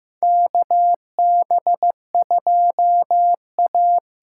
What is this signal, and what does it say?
Morse code, heard as a keyed tone, about 15 wpm, 700 Hz. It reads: KB2A
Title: USA Call Signs - 6 Characters - 8x Character Spacing - 15wpm